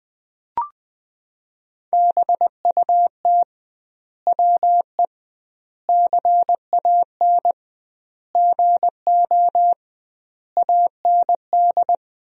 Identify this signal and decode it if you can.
Morse code, heard as a keyed tone, about 20 wpm, 700 Hz. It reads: BUT WE CAN GO AND
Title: 2x Word-Spacing - Sentences from Top 100 Words - 20wpm